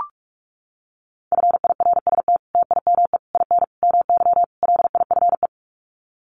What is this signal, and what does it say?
Morse code, heard as a keyed tone, about 44 wpm, 700 Hz. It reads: FIRST TIME IN MY LIFE
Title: Sets of 5 Words Encore2 45wpm